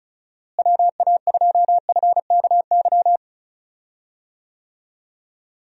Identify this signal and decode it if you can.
Morse code, heard as a keyed tone, about 34 wpm, 700 Hz. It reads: WA2FKY
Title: USA Call Signs - 6 Characters - Repeated 3x - 35wpm